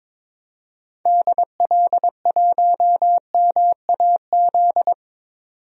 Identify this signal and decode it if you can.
Morse code, heard as a keyed tone, about 22 wpm, 700 Hz. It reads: DL1MAZ
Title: DXCC-100 Call Signs - 6 Characters - 22wpm